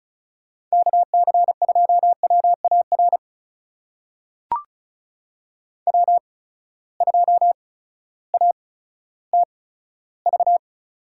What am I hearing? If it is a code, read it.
Morse code, heard as a keyed tone, about 36 wpm, spaced as for 16 wpm, 700 Hz. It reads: KC2WAR W2ATV
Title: USA Call Signs - 8x Character Spacing - 35wpm